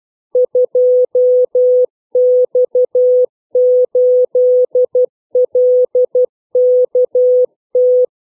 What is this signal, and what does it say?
Morse code, heard as a keyed tone, about 12 wpm, 500 Hz. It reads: 2X8LKT
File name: CW / letter number groups 3